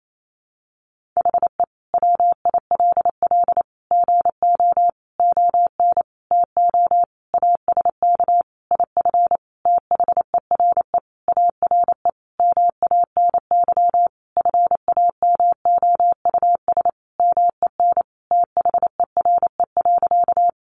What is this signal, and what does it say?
Morse code, heard as a keyed tone, about 28 wpm, 700 Hz. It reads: HE WILL GO ON TO ASK IF THERE ARE MANY FAMOUS MEN THERE.
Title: Sentences from Top 1500 Words 28wpm